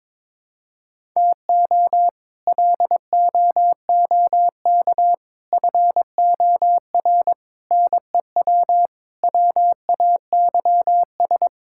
Morse code, heard as a keyed tone, about 23 wpm, 700 Hz. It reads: TO LOOK FOR NEW WAYS
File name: practice / 22wpm / Sets of 6 Words - 22wpm